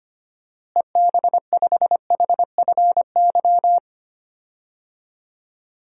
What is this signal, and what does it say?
Morse code, heard as a keyed tone, about 25 wpm, 700 Hz. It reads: EB5HFY
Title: DXCC-100 Call Signs - 6 Characters - Repeated 2x - 25wpm